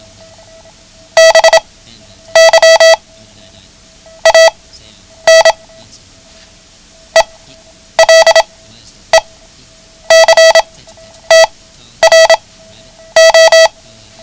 Morse code, heard as a keyed tone, about 27 wpm, spaced as for 10 wpm, 650 Hz. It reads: BY AN ELECTRO
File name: ham radio / CodeQuick / Code Quick 09